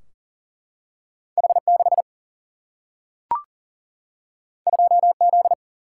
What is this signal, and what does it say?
Morse code, heard as a keyed tone, about 40 wpm, 700 Hz. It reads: H6 2Z
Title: Single-Letter Single-Number 40wpm